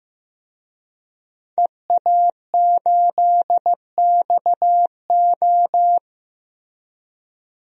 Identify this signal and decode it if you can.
Morse code, heard as a keyed tone, about 16 wpm, 700 Hz. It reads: EA8XO
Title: DXCC-100 Call Signs - 5 Characters - Repeated 3x - 15wpm